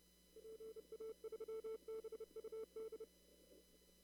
Morse code, heard as a keyed tone, about 30 wpm, 440 Hz. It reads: PA3BUD
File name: CW / kt5j wpx cw-op-k5pi